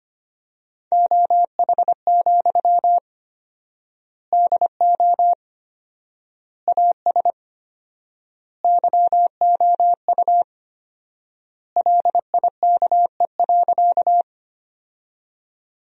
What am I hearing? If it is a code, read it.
Morse code, heard as a keyed tone, about 25 wpm, 700 Hz. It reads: OH, DO AS YOU LIKE.
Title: 4x Word-Spacing - Sentences from Top 500 Words - 25wpm